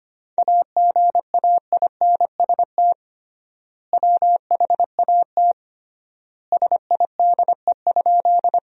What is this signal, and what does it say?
Morse code, heard as a keyed tone, about 25 wpm, 700 Hz. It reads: AGAINST WHAT SIDE?
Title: 3x Word-Spacing - Sentences from Top 400 Words - 25wpm